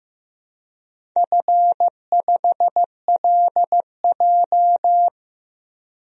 Morse code, heard as a keyed tone, about 15 wpm, 700 Hz. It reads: F5LJ
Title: DXCC-100 Call Signs - 4 Characters - Repeated 2x - 15wpm